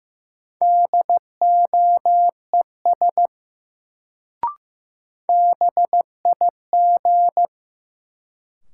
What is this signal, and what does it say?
Morse code, heard as a keyed tone, about 15 wpm, 700 Hz. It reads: DOES BIG
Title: Sentences from Top 200 Words - New Word Drill - 15wpm